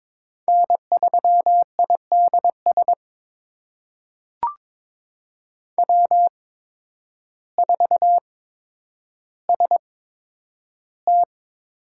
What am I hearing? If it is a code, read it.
Morse code, heard as a keyed tone, about 21 wpm, spaced as for 10 wpm, 700 Hz. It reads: N3IDS W4ST
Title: USA Call Signs - 5 Characters - 8x Character Spacing - 22wpm